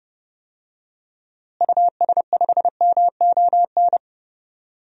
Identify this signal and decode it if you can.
Morse code, heard as a keyed tone, about 30 wpm, 700 Hz. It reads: US5MON
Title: DXCC-100 Call Signs - 6 Characters - Repeated 3x - 30wpm